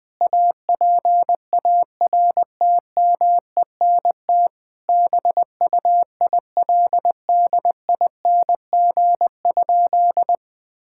Morse code, heard as a keyed tone, about 20 wpm, 700 Hz. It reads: APARTMENT BUILDING?
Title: Sentences from Top 1000 Words - Review Part 6 - 20wpm